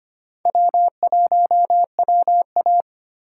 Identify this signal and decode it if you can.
Morse code, heard as a keyed tone, about 25 wpm, 700 Hz. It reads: W1WA
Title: USA Call Signs - 4 Characters - 3x Character Spacing - 25wpm